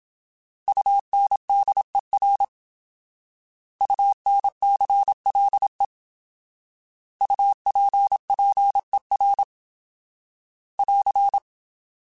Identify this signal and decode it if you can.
Morse code, heard as a keyed tone, about 27 wpm, 800 Hz. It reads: UNDER UNCLE UPPER <AR>
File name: Alphabetical / Alphabetical mp3s / Us @25